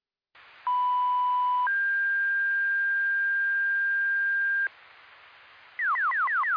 Two-tone page, heard as a one-second tone, then a three-second tone, about 990 then 1600 Hz. It is heard from 0.7 to 4.7 s.